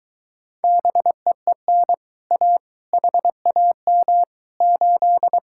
Morse code, heard as a keyed tone, about 23 wpm, 700 Hz. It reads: BEEN A HAM 8